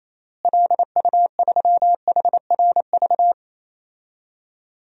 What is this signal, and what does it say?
Morse code, heard as a keyed tone, about 28 wpm, 700 Hz. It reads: LU3HRV
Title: DXCC-100 Call Signs - 6 Characters - Repeated 3x - 28wpm